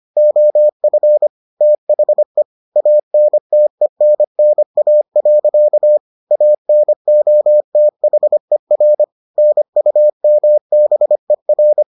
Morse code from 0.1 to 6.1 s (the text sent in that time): OF THE ANTENNA.